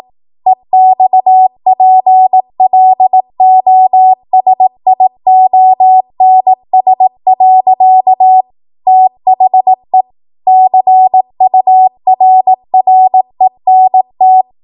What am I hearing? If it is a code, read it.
Morse code, heard as a keyed tone, about 18 wpm, 750 Hz. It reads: EXPLOSIONS. THE CURRENT
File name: Morse / 18WPM